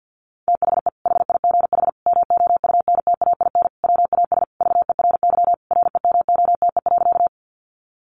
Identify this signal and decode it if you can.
Morse code, heard as a keyed tone, about 48 wpm, 700 Hz. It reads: THE HIGH MOUNTAIN WAS VERY REMOTE.
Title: Sentences from Top 1300 Words 50wpm